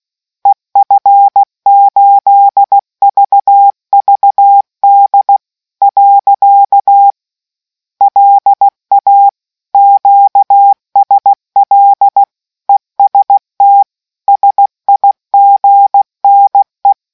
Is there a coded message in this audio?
Yes: Morse code from 0.5 to 16.9 s, heard as a keyed tone, about 16 wpm, 800 Hz. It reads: EF8VVD . LA QSL EST SIGNE